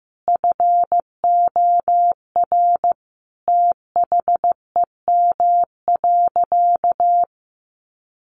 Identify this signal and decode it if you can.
Morse code, heard as a keyed tone, about 15 wpm, 700 Hz. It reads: FOR THEM.
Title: Sentences from Top 900 Words 15wpm